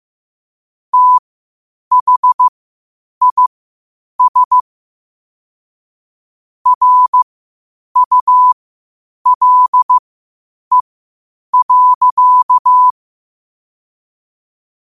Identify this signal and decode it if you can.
Morse code, heard as a keyed tone, about 15 wpm, spaced as for 8 wpm, 1000 Hz. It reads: THIS RULE.
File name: qotdfiles / 8wpm